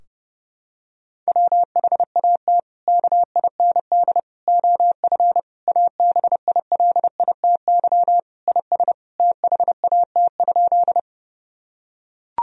Morse code, heard as a keyed tone, about 32 wpm, 700 Hz. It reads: WHAT KIND OF ABILITY IS THAT?
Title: Sentences from Top 1000 Words - Review Part 3 - 30wpm